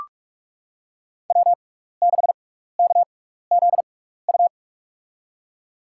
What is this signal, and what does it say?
Morse code, heard as a keyed tone, about 46 wpm, spaced as for 15 wpm, 700 Hz. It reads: W6KZU
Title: USA Call Signs - 6x Character Spacing - 45wpm